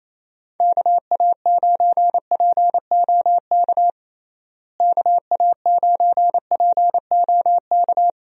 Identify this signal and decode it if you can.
Morse code, heard as a keyed tone, about 28 wpm, 700 Hz. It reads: KA9POK KA9POK
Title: USA Call Signs - 6 Characters - Repeated 3x - 28wpm